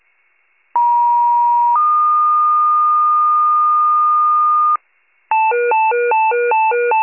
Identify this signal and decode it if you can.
Two-tone page, heard as a one-second tone, then a three-second tone, about 950 then 1220 Hz, with noise. It is heard from 0.7 to 4.8 s.